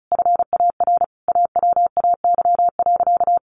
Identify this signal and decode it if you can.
Morse code, heard as a keyed tone, about 34 wpm, 700 Hz. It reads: FAR AWAY.
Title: Sentences from Top 1100 Words 35wpm